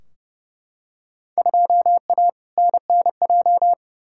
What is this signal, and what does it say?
Morse code, heard as a keyed tone, about 32 wpm, 700 Hz. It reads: 2A NNJ
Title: Field Day Exchanges - 30wpm